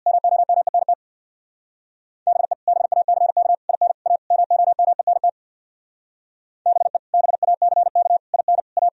Morse code, heard as a keyed tone, about 64 wpm, 700 Hz. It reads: MOMENT BE BACK IN A MOMENT BE BACK IN A